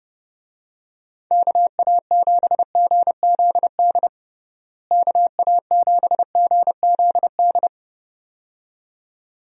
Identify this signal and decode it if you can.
Morse code, heard as a keyed tone, about 30 wpm, 700 Hz. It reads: KA7GZD KA7GZD
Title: USA Call Signs - 6 Characters - Repeated 2x - 30wpm